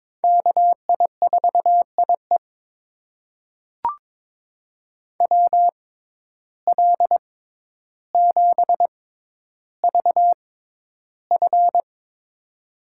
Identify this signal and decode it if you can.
Morse code, heard as a keyed tone, about 22 wpm, spaced as for 11 wpm, 700 Hz. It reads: KI4IE WL7VF
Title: USA Call Signs - 5 Characters - 6x Character Spacing - 22wpm